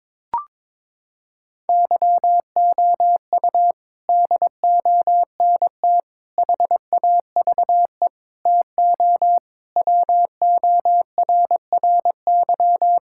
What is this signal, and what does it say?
Morse code, heard as a keyed tone, about 22 wpm, 700 Hz. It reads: YOU DONT HAVE TO WORRY
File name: practice / 22wpm / Sets of 6 Words - 22wpm